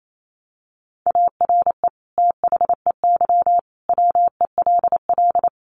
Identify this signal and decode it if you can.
Morse code, heard as a keyed tone, about 28 wpm, 700 Hz. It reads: ARE THEY WELL
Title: Sentences from Top 1300 Words 28wpm